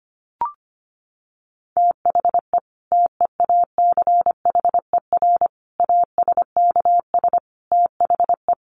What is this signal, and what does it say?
Morse code, heard as a keyed tone, about 25 wpm, 700 Hz. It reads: THE TEACHER ASKS THE
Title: Additional Easy Phrases 25wpm